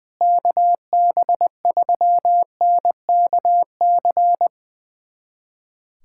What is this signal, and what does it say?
Morse code, heard as a keyed tone, about 20 wpm, 700 Hz. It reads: KB3NKC
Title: USA Callsign DE Callsign - 6 Characters - 20wpm